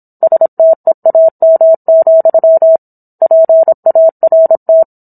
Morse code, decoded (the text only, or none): STEAM, PART